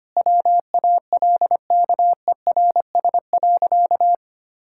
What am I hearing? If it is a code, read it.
Morse code, heard as a keyed tone, about 25 wpm, 700 Hz. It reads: WALKERS.
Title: Pangram Challenge v2 - 25wpm